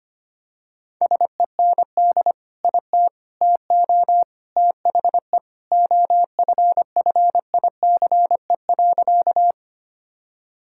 Morse code, heard as a keyed tone, about 25 wpm, 700 Hz. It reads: SEND IT TO THE OFFICE.
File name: practice / 25wpm / Sentences with 3-Word Phrases - 25wpm